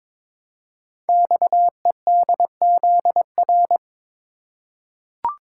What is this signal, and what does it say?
Morse code, heard as a keyed tone, about 22 wpm, 700 Hz. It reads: XEDZR
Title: Code Groups - 5 Letters - 22wpm